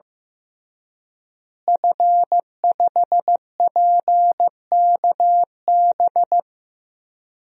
Morse code, heard as a keyed tone, about 15 wpm, 700 Hz. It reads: F5PKB
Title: DXCC-100 Call Signs - 5 Characters - Repeated 3x - 15wpm